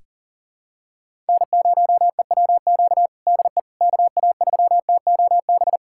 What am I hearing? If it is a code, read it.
Morse code, heard as a keyed tone, about 40 wpm, 700 Hz. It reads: N0EWQ DE KA3TOB
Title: USA Callsign DE Callsign 40wpm